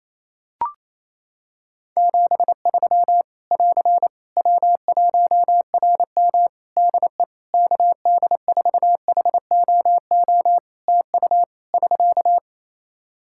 Morse code, heard as a keyed tone, about 27 wpm, 700 Hz. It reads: 73 <AR> W1RM DE KD4HOO TU <SK>
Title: Practice QSO - Exchange 4 - Ending - 28wpm